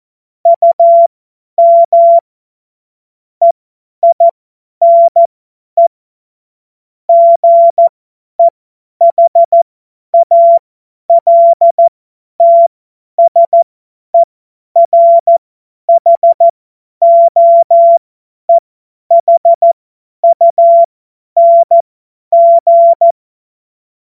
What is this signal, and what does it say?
Morse code, heard as a keyed tone, about 14 wpm, spaced as for 10 wpm, 680 Hz. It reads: UM EINE GEHALTSERHOEHUNG
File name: Morsekurs / Witze / Witz 27. Dezember 2022 50 F70